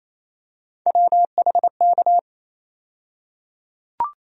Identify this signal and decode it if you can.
Morse code, heard as a keyed tone, about 28 wpm, 700 Hz. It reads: WHK
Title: USA Call Sign Postfixes - 3 Letters 28wpm